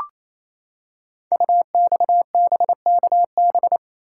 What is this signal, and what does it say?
Morse code, heard as a keyed tone, about 28 wpm, 700 Hz. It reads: UXBKB
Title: Code Groups - 5 Letters - 28wpm